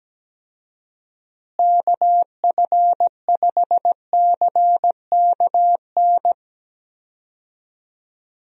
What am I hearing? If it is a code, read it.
Morse code, heard as a keyed tone, about 17 wpm, 700 Hz. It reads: KF5CKN
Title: USA Call Signs - 6 Characters - Repeated 2x - 17wpm